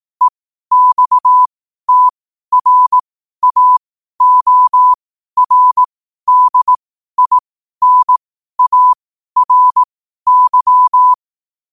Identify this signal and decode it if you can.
Morse code, heard as a keyed tone, about 18 wpm, spaced as for 12 wpm, 1000 Hz. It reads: EXTRAORDINARY
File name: qotdfiles / 12wpm